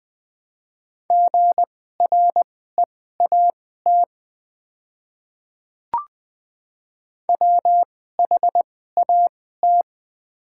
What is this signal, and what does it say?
Morse code, heard as a keyed tone, about 20 wpm, spaced as for 14 wpm, 700 Hz. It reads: GREAT WHAT
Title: 2x Character Spacing - Top 200 Words - 20wpm